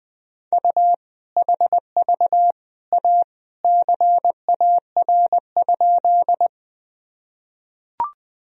Morse code, heard as a keyed tone, about 21 wpm, 700 Hz. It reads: U HV A CAR?
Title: Sentences from Top 300 Words and Abbreviations 20wpm